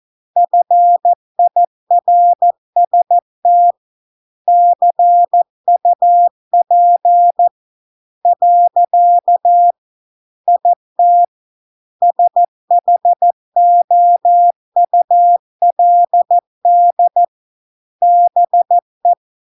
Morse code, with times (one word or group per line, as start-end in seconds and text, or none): 0.4-3.7 FIRST
4.5-7.5 CUP
8.3-9.7 .
10.5-11.2 IT
12.0-17.2 SHOULD
18.0-19.1 BE